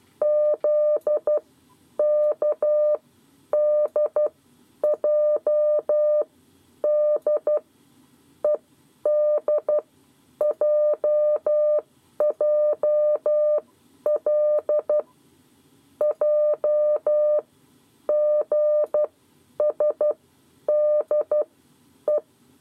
Morse code, heard as a keyed tone, about 12 wpm, spaced as for 9 wpm, 600 Hz. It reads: ZKDJD EDJJL JGSDE
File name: 7 Lezione cw 2